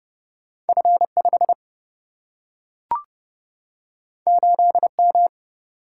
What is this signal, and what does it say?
Morse code, heard as a keyed tone, about 30 wpm, 700 Hz. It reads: F5 8M
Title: Single-Letter Single-Number 30wpm